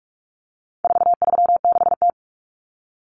Morse code, heard as a keyed tone, about 44 wpm, 700 Hz. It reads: 436T
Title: Common Cut Numbers - 4-Digit 45wpm